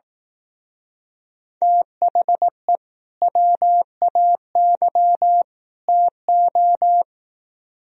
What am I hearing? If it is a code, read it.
Morse code, heard as a keyed tone, about 17 wpm, 700 Hz. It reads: THE WAY TO
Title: Speed-Racing - Sets of 3 Words 15wpm